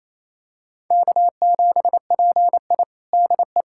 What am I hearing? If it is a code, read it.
Morse code, heard as a keyed tone, about 28 wpm, 700 Hz. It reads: K7PI DE
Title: USA Callsign DE Callsign - 4 Characters - 28wpm